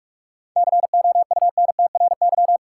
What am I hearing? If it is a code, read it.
Morse code, heard as a keyed tone, about 41 wpm, 700 Hz. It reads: COUNTRY
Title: Sentences from Top 300 Words - New Word Drill - 45wpm